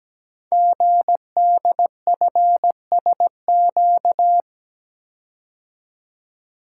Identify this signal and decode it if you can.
Morse code, heard as a keyed tone, about 17 wpm, 700 Hz. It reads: GDFSQ
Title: Code Groups - 5 Letters - 17wpm